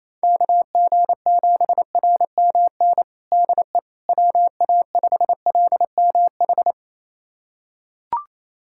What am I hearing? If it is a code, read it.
Morse code, heard as a keyed tone, about 28 wpm, 700 Hz. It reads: KG7RMN DE WA5LMH
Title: USA Callsign DE Callsign - 6 Characters - 28wpm